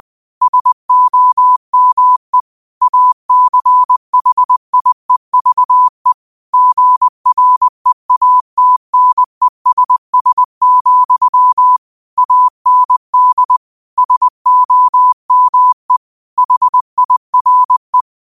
Morse code, heard as a keyed tone, about 20 wpm, 1000 Hz. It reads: SOME ACHIEVE GREATNESS, AND SOME HIRE
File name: qotdfiles / 20wpm